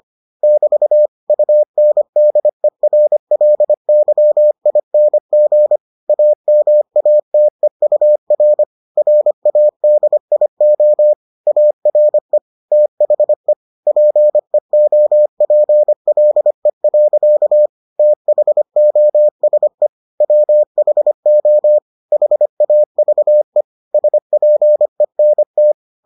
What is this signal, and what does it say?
Morse code, heard as a keyed tone, about 25 wpm, 600 Hz. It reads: <BT> UNDERLYING AMATEUR RADIO ARE THE PEOPLE. THOSE WHO HAVE SPENT